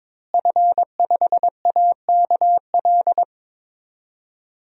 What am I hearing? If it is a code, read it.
Morse code, heard as a keyed tone, about 22 wpm, 700 Hz. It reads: F5AKL
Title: DXCC-100 Call Signs - 5 Characters - Repeated 2x - 22wpm